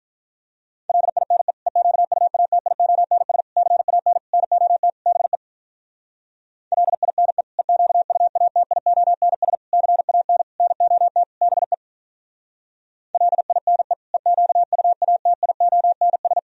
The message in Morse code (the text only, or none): LINE EQUATIONS CAN NOT BE LINE EQUATIONS CAN NOT BE LINE EQUATIONS